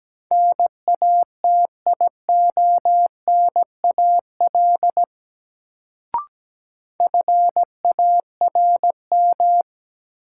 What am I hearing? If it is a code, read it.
Morse code, heard as a keyed tone, about 17 wpm, 700 Hz. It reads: NATIONAL FARM
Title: Sentences from Top 1200 Words - New Word Drill Alt - 17wpm